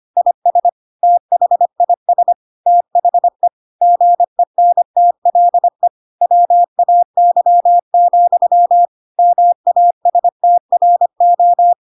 Morse code, heard as a keyed tone, about 25 wpm, 700 Hz. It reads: IS THIS THE GENTLE WAY, MASTRO
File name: Pinocchio / Pinocchio ch2 25 25 ik1hge